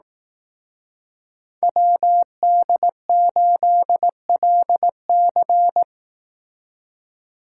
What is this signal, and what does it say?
Morse code, heard as a keyed tone, about 18 wpm, 700 Hz. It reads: WD8LC
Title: Speed Racing - USA 2x2 Call Signs 15wpm